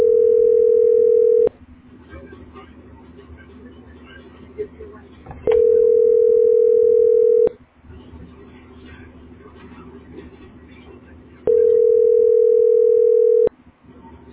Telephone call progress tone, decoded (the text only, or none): ringback tone